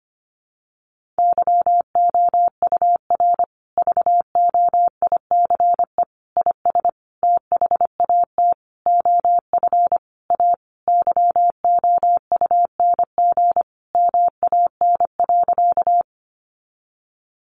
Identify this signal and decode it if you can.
Morse code, heard as a keyed tone, about 25 wpm, 700 Hz. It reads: YOUR VOICE IS THAT OF A YOUNG MAN.
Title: Sentences from Top 500 Words 25wpm